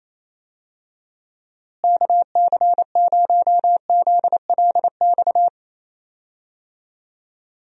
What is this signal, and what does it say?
Morse code, heard as a keyed tone, about 28 wpm, 700 Hz. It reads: KC0ZLX
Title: USA Call Signs - 6 Characters - Repeated 2x - 28wpm